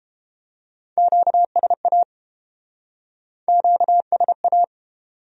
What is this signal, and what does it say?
Morse code, heard as a keyed tone, about 32 wpm, 700 Hz. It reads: QSA QSA